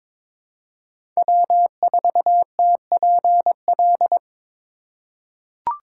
Morse code, heard as a keyed tone, about 22 wpm, 690 Hz. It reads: W4TPL
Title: USA Call Signs - 5 Characters - 2x Character Spacing - 22wpm